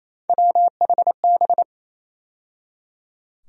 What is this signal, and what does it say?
Morse code, heard as a keyed tone, about 28 wpm, 700 Hz. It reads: WHB
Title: USA Call Sign Postfixes - 3 Letters 28wpm